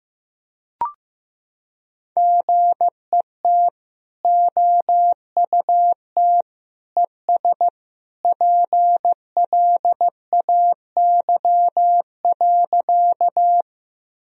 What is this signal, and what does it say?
Morse code, heard as a keyed tone, about 15 wpm, 700 Hz. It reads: GET OUT ES PLAY.
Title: Sentences from Top 200 Words and Abbreviations 15wpm